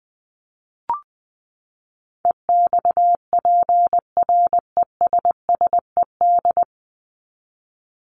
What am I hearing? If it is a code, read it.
Morse code, heard as a keyed tone, about 20 wpm, 700 Hz. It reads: EXPRESSED
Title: ED Suffix - Top 100 Words with Base Words 20wpm